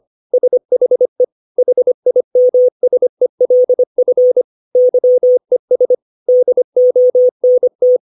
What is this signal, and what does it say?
Morse code, heard as a keyed tone, about 25 wpm, 500 Hz. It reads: SHE HIMSELF YES DONT
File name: 632 words / 632 Words 25WPM